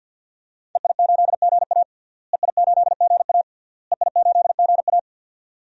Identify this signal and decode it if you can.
Morse code, heard as a keyed tone, about 50 wpm, 700 Hz. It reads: EI8GA EI8GA EI8GA
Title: DXCC-100 Call Signs - 5 Characters - Repeated 3x - 50wpm